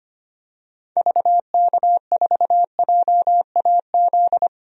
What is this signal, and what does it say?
Morse code, heard as a keyed tone, about 25 wpm, 690 Hz. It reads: VK4JAZ
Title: DXCC-100 Call Signs - 6 Characters - Repeated 3x - 25wpm